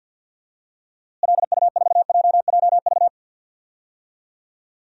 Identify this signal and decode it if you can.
Morse code, heard as a keyed tone, about 52 wpm, 700 Hz. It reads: LU4JJV
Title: DXCC-100 Call Signs - 6 Characters - Repeated 3x - 50wpm